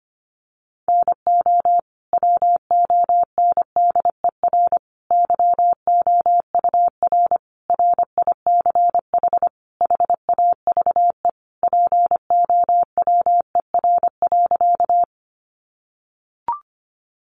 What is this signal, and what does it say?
Morse code, heard as a keyed tone, about 25 wpm, 700 Hz. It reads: NO WONDER YOUR RICH HAVE POWER.
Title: Sentences from Top 1100 Words 25wpm